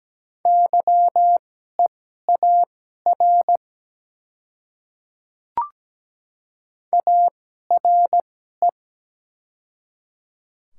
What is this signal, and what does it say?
Morse code, heard as a keyed tone, about 17 wpm, spaced as for 12 wpm, 690 Hz. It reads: YEAR ARE
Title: 2x Character Spacing - Top 300 Words - 17wpm